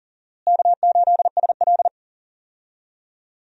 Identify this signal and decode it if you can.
Morse code, heard as a keyed tone, about 40 wpm, 700 Hz. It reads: K8SL
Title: USA Call Signs - 4 Characters - Repeated 2x - 40wpm